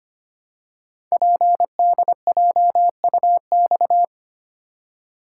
Morse code, heard as a keyed tone, about 25 wpm, 700 Hz. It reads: PDJUX
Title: Code Groups - 5 Letters - 25wpm